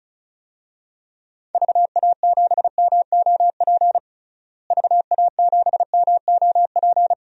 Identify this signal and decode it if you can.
Morse code, heard as a keyed tone, about 36 wpm, 700 Hz. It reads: VA7MOP VA7MOP
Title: DXCC-100 Call Signs - 6 Characters - Repeated 3x - 35wpm